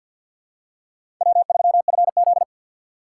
Morse code, heard as a keyed tone, about 49 wpm, 700 Hz. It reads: W3FZ
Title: USA Call Signs - 6x Character Spacing - 50wpm